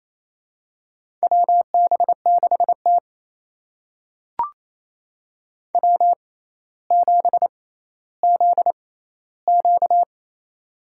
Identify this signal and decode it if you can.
Morse code, heard as a keyed tone, about 28 wpm, spaced as for 14 wpm, 700 Hz. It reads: WB6T W7ZQ
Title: USA Call Signs - 4 Characters - 6x Character Spacing - 28wpm